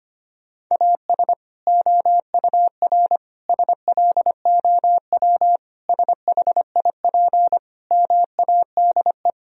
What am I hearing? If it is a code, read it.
Morse code, heard as a keyed tone, about 25 wpm, 700 Hz. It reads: AS OUR SLOW SHIP MADE
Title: Sentences from Top 1000 Words - Review Part 2 - 25wpm